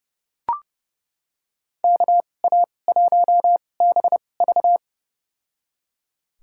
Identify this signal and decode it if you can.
Morse code, heard as a keyed tone, about 30 wpm, spaced as for 22 wpm, 700 Hz. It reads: KA1BV
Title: USA Call Signs - 5 Characters - 2x Character Spacing - 30wpm